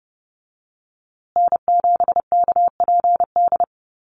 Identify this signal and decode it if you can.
Morse code, heard as a keyed tone, about 30 wpm, 700 Hz. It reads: N7KPD
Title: USA Call Signs Repeated 3x 30wpm